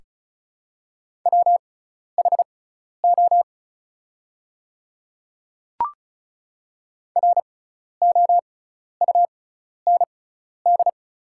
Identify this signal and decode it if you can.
Morse code, heard as a keyed tone, about 36 wpm, spaced as for 12 wpm, 700 Hz. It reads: WHO ROUND
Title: 6x Character Spacing - Top 200 Words - 35wpm